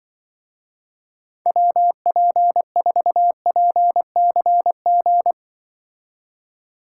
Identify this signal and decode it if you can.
Morse code, heard as a keyed tone, about 24 wpm, 700 Hz. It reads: WP4PCG
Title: Speed Racing - USA 2x3 Call Signs 20wpm